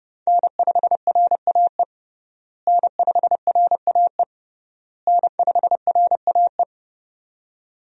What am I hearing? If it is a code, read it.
Morse code, heard as a keyed tone, about 32 wpm, 700 Hz. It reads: N5RAE N5RAE N5RAE
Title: USA Call Signs - 5 Characters - Repeated 3x - 30wpm